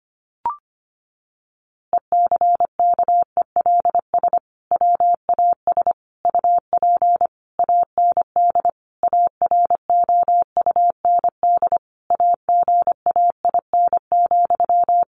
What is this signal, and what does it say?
Morse code, heard as a keyed tone, about 25 wpm, 700 Hz. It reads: ECKELS WAS UP AND AROUND AGAIN,
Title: A Sound Of Thunder 25wpm